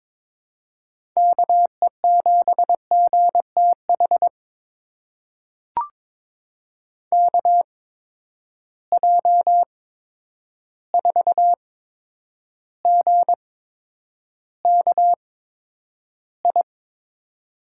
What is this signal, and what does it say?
Morse code, heard as a keyed tone, about 22 wpm, spaced as for 9 wpm, 700 Hz. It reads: KE7GTH KJ4GKI
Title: USA Call Signs - 8x Character Spacing - 22wpm